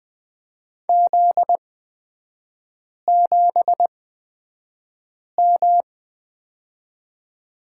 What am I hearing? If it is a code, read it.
Morse code, heard as a keyed tone, about 20 wpm, spaced as for 5 wpm, 700 Hz. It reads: Z7M
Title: 243 Lesson 39e - Letter z - Three Characters 20wpm